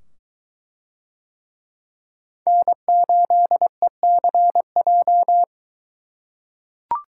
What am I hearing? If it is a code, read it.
Morse code, heard as a keyed tone, about 21 wpm, 700 Hz. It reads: N8ECJ